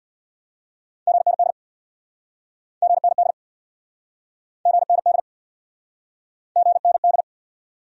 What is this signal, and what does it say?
Morse code, heard as a keyed tone, about 69 wpm, 700 Hz. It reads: GND GND GND GND